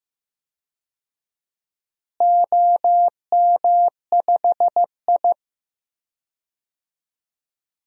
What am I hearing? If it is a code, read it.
Morse code, heard as a keyed tone, about 16 wpm, 700 Hz. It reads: OM5I
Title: DXCC-100 Call Signs - 4 Characters - Repeated 3x - 15wpm